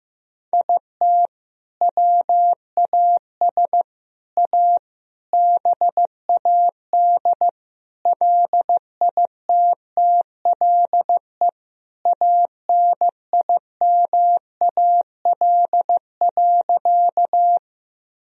Morse code, heard as a keyed tone, about 15 wpm, 700 Hz. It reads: IT WAS A BAD LITTLE ANIMAL.
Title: Sentences from Top 1000 Words - Review Part 1 - 15wpm